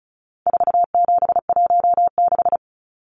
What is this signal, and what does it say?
Morse code, heard as a keyed tone, about 34 wpm, 700 Hz. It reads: 4716
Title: Numbers - 4-Digit 35wpm